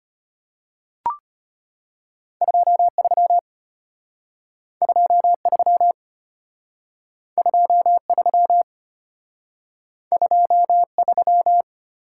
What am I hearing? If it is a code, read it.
Morse code, heard as a keyed tone, about 32 wpm, 700 Hz. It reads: 23 23 23 23